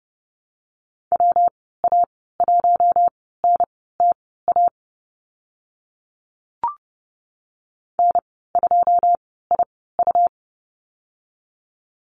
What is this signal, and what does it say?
Morse code, heard as a keyed tone, about 28 wpm, spaced as for 17 wpm, 700 Hz. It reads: WA1NTA N2IU
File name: Lesson 19g - Letter w - Call Signs 30wpm